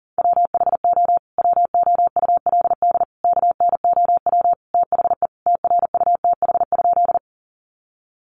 Morse code, heard as a keyed tone, about 40 wpm, 700 Hz. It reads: WHO WOULD KNOW THE TRUTH?
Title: Sentences from Top 800 Words 40wpm